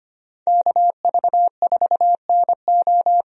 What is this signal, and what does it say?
Morse code, heard as a keyed tone, about 25 wpm, 700 Hz. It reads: KV4NO
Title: USA Call Signs - 5 Characters - 25wpm